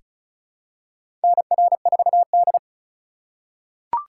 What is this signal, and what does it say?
Morse code, heard as a keyed tone, about 36 wpm, 700 Hz. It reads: NR4D
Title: USA Call Signs - 4 Characters - 3x Character Spacing - 35wpm